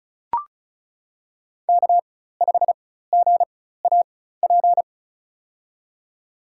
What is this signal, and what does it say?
Morse code, heard as a keyed tone, about 36 wpm, spaced as for 16 wpm, 700 Hz. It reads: K5GAP
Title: USA Call Signs - 5 Characters - 4x Character Spacing - 35wpm